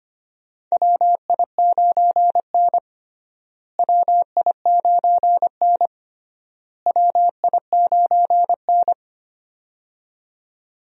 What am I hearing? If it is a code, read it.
Morse code, heard as a keyed tone, about 25 wpm, 700 Hz. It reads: WI9N WI9N WI9N
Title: USA Call Signs - 4 Characters - Repeated 3x - 25wpm